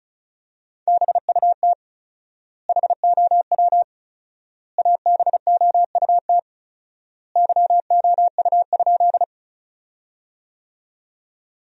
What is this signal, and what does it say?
Morse code, heard as a keyed tone, about 36 wpm, 700 Hz. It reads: BUT HOW ABOUT YOU?
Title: 4x Word-Spacing - Sentences from Top 100 Words - 35wpm